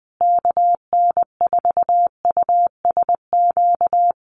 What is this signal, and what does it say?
Morse code, heard as a keyed tone, about 20 wpm, 700 Hz. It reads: KN4USQ
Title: USA Call Signs Repeated 3x 20wpm